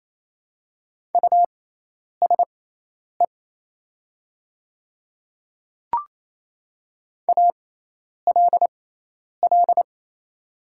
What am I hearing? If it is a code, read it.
Morse code, heard as a keyed tone, about 27 wpm, spaced as for 10 wpm, 700 Hz. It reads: USE ALL
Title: 6x Character Spacing - Top 100 Words - 28wpm